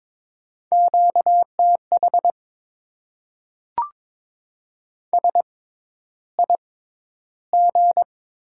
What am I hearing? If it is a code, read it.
Morse code, heard as a keyed tone, about 22 wpm, spaced as for 11 wpm, 700 Hz. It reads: QTH SIG
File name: Top 90 QSO Elements - 6x Character Spacing - 22wpm